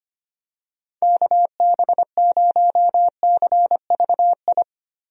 Morse code, heard as a keyed tone, about 25 wpm, 700 Hz. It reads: KB0CVI
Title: USA Call Signs - 6 Characters - Repeated 2x - 25wpm